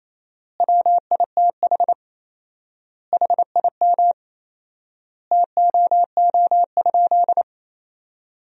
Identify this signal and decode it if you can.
Morse code, heard as a keyed tone, about 28 wpm, 700 Hz. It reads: WITH HIM TOO?
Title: 4x Word-Spacing - Sentences from Top 500 Words - 28wpm